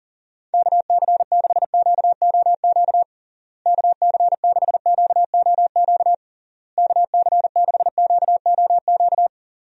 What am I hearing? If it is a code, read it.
Morse code, heard as a keyed tone, about 40 wpm, 700 Hz. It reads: KC6QOQ KC6QOQ KC6QOQ
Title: USA Call Signs - 6 Characters - Repeated 3x - 40wpm